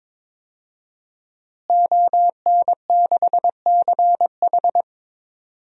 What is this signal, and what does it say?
Morse code, heard as a keyed tone, about 21 wpm, 700 Hz. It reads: ON6CH
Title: DXCC-100 Call Signs - 5 Characters - Repeated 3x - 22wpm